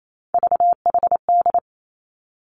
Morse code, heard as a keyed tone, about 28 wpm, 700 Hz. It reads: VHD
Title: Three Challenging Letter-Number Combinations 28wpm